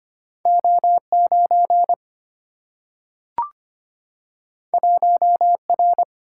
Morse code, heard as a keyed tone, about 25 wpm, 700 Hz. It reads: O9 1R
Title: Single-Letter Single-Number 25wpm